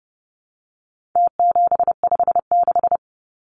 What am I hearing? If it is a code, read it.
Morse code, heard as a keyed tone, about 30 wpm, 700 Hz. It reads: T756
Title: Numbers - Contest Style - 4-Digit 30wpm